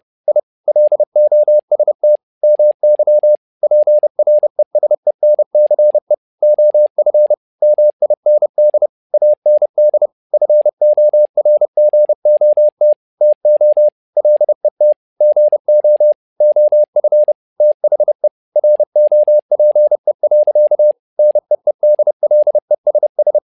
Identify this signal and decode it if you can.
Morse code, heard as a keyed tone, about 30 wpm, 600 Hz. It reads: I LOST MY PRESENCE OF MIND AND FORGOT TO LET GO OF THE ROPE. NEEDLESS
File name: learning center / tower trouble / tower trouble 30wpm